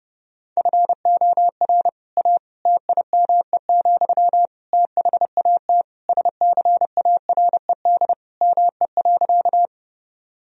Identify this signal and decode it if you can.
Morse code, heard as a keyed tone, about 30 wpm, 700 Hz. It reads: FOR A TIME, THAT SCARED ME.
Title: Sentences from Top 1000 Words - Review Part 4 - 30wpm